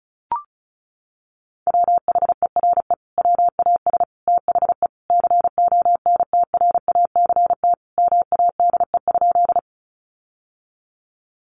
Morse code, heard as a keyed tone, about 34 wpm, 700 Hz. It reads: WHERE WAS THE CONTRACT MADE?
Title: Sentences from Top 1400 Words 35wpm